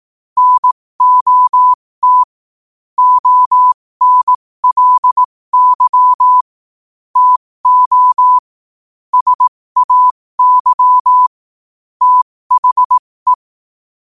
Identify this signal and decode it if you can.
Morse code, heard as a keyed tone, about 18 wpm, spaced as for 15 wpm, 1000 Hz. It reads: NOT ONLY TO SAY THE
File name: qotdfiles / 15wpm